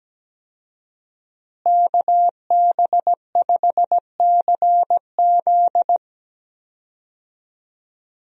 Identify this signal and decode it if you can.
Morse code, heard as a keyed tone, about 17 wpm, 700 Hz. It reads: KB5CZ
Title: USA Call Signs - 5 Characters - Repeated 2x - 17wpm